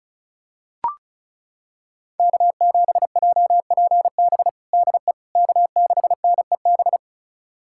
Morse code, heard as a keyed tone, about 36 wpm, 700 Hz. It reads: K7JPB DE K6NEB
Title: USA Callsign DE Callsign 35wpm